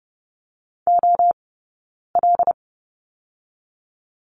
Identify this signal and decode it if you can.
Morse code, heard as a keyed tone, about 30 wpm, 700 Hz. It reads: O L
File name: 092 Lesson 16d - Letter c - Two Characters 30wpm